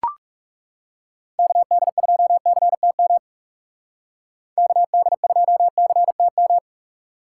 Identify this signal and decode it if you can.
Morse code, heard as a keyed tone, about 39 wpm, 700 Hz. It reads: KD2CTM KD2CTM